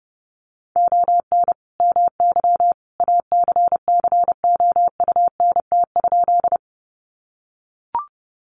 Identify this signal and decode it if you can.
Morse code, heard as a keyed tone, about 28 wpm, 700 Hz. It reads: ON MY ACCOUNT?
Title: Sentences from Top 900 Words 30wpm